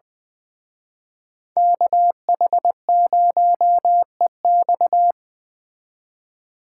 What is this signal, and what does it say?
Morse code, heard as a keyed tone, about 23 wpm, 700 Hz. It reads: KH0EX
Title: Speed Racing - USA 2x2 Call Signs 20wpm